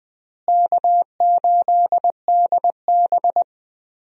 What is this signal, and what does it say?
Morse code, heard as a keyed tone, about 21 wpm, 700 Hz. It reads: K8DB
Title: USA Call Signs - 4 Characters - Repeated 2x - 20wpm